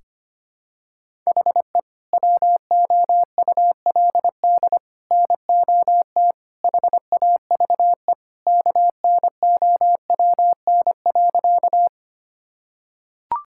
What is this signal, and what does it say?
Morse code, heard as a keyed tone, about 25 wpm, 700 Hz. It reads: HE WOULD NOT HAVE KNOWN.
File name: practice / 25wpm / Sentences with 3-Word Phrases - 25wpm